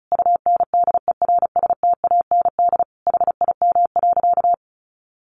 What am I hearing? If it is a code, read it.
Morse code, heard as a keyed tone, about 34 wpm, 700 Hz. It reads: UNDERSTAND HIM.
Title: Sentences from Top 600 Words 35wpm